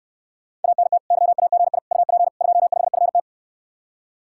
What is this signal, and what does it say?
Morse code, heard as a keyed tone, about 69 wpm, 700 Hz. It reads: ANT YAGI UP 25FT